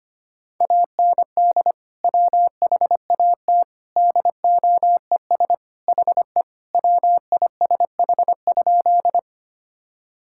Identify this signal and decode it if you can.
Morse code, heard as a keyed tone, about 25 wpm, 700 Hz. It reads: AND WHAT DOES HE WISH?
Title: Sentences from Top 1600 Words 25wpm